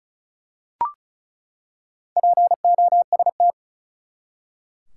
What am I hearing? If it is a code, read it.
Morse code, heard as a keyed tone, about 34 wpm, 700 Hz. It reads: POST
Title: Four Letter Words - 35wpm